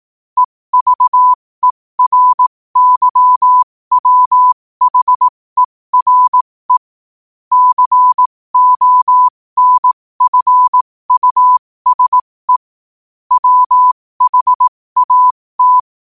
Morse code, heard as a keyed tone, about 18 wpm, spaced as for 15 wpm, 1000 Hz. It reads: EVERYWHERE CONFUSE WHAT
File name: qotdfiles / 15wpm